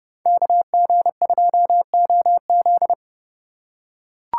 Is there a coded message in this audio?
Yes: Morse code from 0.3 to 2.9 s, heard as a keyed tone, about 29 wpm, 700 Hz. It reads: KG2OZ